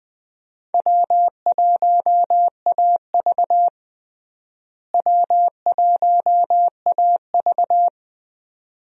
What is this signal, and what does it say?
Morse code, heard as a keyed tone, about 20 wpm, 700 Hz. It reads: W1AV W1AV
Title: USA Call Signs - 4 Characters - Repeated 2x - 20wpm